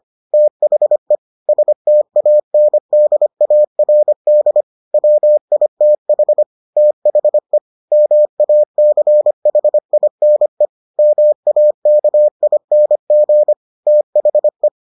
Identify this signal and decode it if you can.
Morse code, heard as a keyed tone, about 25 wpm, 600 Hz. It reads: THE STANDARD WITH THE MACHINE MAKING THE